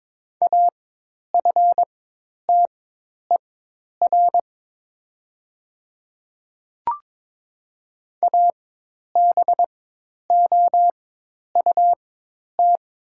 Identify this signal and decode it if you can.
Morse code, heard as a keyed tone, about 22 wpm, spaced as for 10 wpm, 700 Hz. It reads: AFTER ABOUT
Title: 4x Character Spacing - Top 500 Words - 22wpm